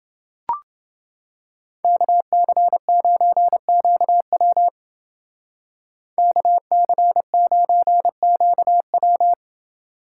Morse code, heard as a keyed tone, about 29 wpm, 700 Hz. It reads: KC9QW KC9QW